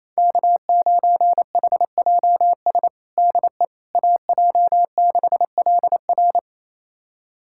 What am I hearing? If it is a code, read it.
Morse code, heard as a keyed tone, about 28 wpm, 700 Hz. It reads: K9HJS DE AJ6LR
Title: USA Callsign DE Callsign - 5 Characters - 28wpm